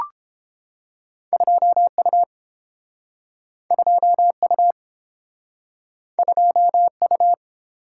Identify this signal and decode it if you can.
Morse code, heard as a keyed tone, about 32 wpm, 700 Hz. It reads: 2U 2U 2U